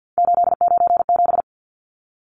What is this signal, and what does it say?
Morse code, heard as a keyed tone, about 50 wpm, 700 Hz. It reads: 897
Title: Numbers - 3-Digit 50wpm